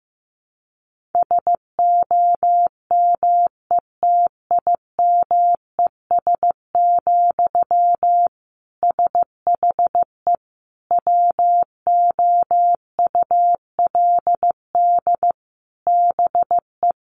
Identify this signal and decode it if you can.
Morse code, heard as a keyed tone, about 15 wpm, 700 Hz. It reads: SOMETIMES, SHE WOULD BE
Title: Sentences from Top 1000 Words 15wpm